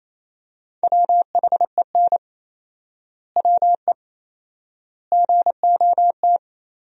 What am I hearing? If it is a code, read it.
Morse code, heard as a keyed tone, about 28 wpm, 700 Hz. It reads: WHEN WE GOT
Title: 4x Word-Spacing - Sentences from Top 500 Words - 28wpm